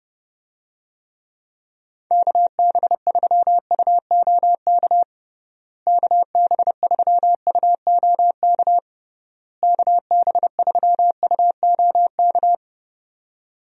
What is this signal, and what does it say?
Morse code, heard as a keyed tone, about 32 wpm, 700 Hz. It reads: KB3UOK KB3UOK KB3UOK
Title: USA Call Signs - 6 Characters - Repeated 3x - 30wpm